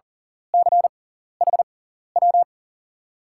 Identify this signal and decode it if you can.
Morse code, heard as a keyed tone, about 40 wpm, spaced as for 14 wpm, 700 Hz. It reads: CHW
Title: USA Call Signs - 6x Character Spacing - 40wpm